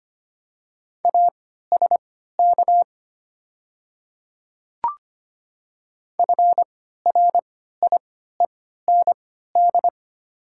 Morse code, heard as a keyed tone, about 25 wpm, spaced as for 14 wpm, 700 Hz. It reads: ASK FRIEND
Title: 3x Character Spacing - Top 500 Words - 25wpm